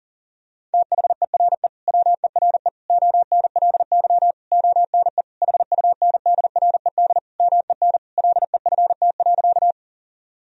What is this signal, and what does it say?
Morse code, heard as a keyed tone, about 40 wpm, 700 Hz. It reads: THERE WERE ONLY ONE HUNDRED MEN LEFT.
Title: Sentences from Top 1000 Words - Review Part 2 - 40wpm